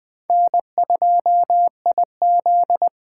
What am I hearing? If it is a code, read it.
Morse code, heard as a keyed tone, about 23 wpm, 700 Hz. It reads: N2IZ
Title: Speed Racing - USA 1x2 Call Signs 20wpm